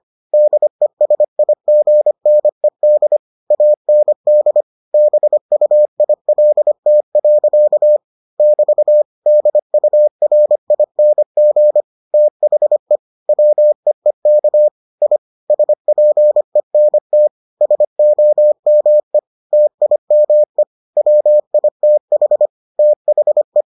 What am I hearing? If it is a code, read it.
Morse code, heard as a keyed tone, about 25 wpm, 600 Hz. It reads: DESIGNED AND BUILT. <BT> DURING THE WEEK I SPENT SOME TIME WITH THE